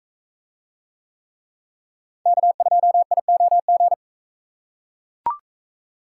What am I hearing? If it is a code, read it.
Morse code, heard as a keyed tone, about 43 wpm, 700 Hz. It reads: K2IOG